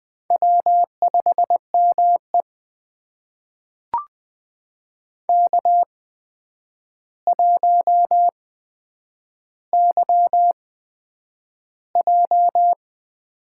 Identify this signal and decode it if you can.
Morse code, heard as a keyed tone, about 20 wpm, spaced as for 9 wpm, 700 Hz. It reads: W5ME K1YJ
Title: USA Call Signs - 4 Characters - 8x Character Spacing - 20wpm